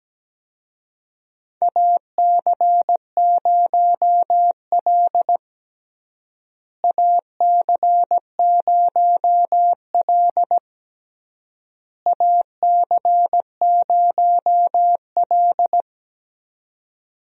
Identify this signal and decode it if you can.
Morse code, heard as a keyed tone, about 17 wpm, 700 Hz. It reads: AC0L AC0L AC0L
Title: USA Call Signs - 4 Characters - Repeated 3x - 17wpm